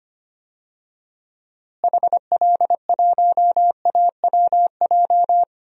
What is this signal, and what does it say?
Morse code, heard as a keyed tone, about 25 wpm, 700 Hz. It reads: HL1AWJ
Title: DXCC-100 Call Signs - 6 Characters - Repeated 2x - 25wpm